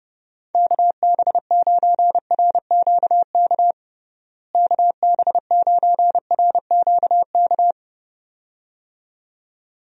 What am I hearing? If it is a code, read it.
Morse code, heard as a keyed tone, about 30 wpm, 700 Hz. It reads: KB9RQK KB9RQK
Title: USA Call Signs - 6 Characters - Repeated 2x - 30wpm